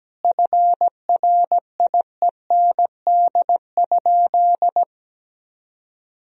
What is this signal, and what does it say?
Morse code, heard as a keyed tone, about 17 wpm, 700 Hz. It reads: FRIEND?
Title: Sentences from Top 300 Words and Abbreviations 17wpm